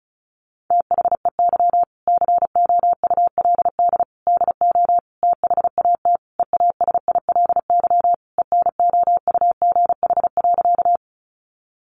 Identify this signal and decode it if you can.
Morse code, heard as a keyed tone, about 34 wpm, 700 Hz. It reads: THEY COULD DO THAT EASILY ENOUGH.
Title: Sentences from Top 900 Words 35wpm